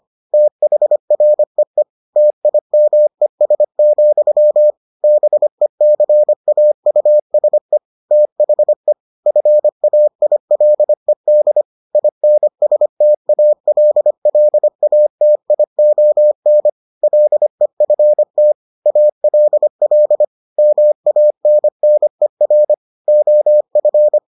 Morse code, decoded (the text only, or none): THREE TIMES, BECAUSE THE FAILED INSTALLATION LEFT ALL MANNER OF